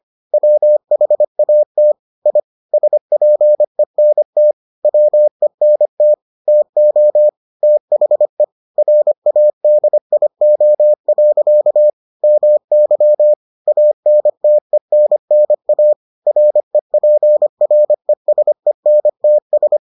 Morse code, heard as a keyed tone, about 25 wpm, 600 Hz. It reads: WHAT I SPENT WENT TO THE RADIO. MY ANTENNA REPRESENTS